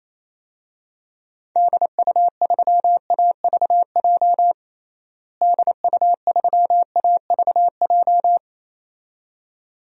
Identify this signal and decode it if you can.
Morse code, heard as a keyed tone, about 27 wpm, 700 Hz. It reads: DU3AVJ DU3AVJ
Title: DXCC-100 Call Signs - 6 Characters - Repeated 2x - 28wpm